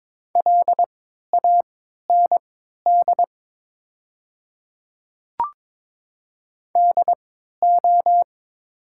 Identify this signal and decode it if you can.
Morse code, heard as a keyed tone, about 22 wpm, spaced as for 12 wpm, 690 Hz. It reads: LAND DO
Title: 3x Character Spacing - Top 200 Words - 22wpm